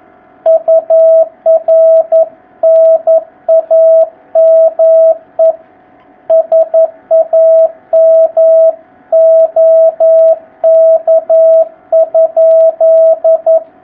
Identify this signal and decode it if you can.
Morse code, heard as a keyed tone, about 11 wpm, 660 Hz. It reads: UR NAME SAM OK?
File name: jh8cbh / 2019cw